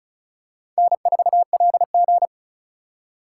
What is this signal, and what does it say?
Morse code, heard as a keyed tone, about 36 wpm, 700 Hz. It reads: N4LG
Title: USA Call Signs - 2x Character Spacing - 35wpm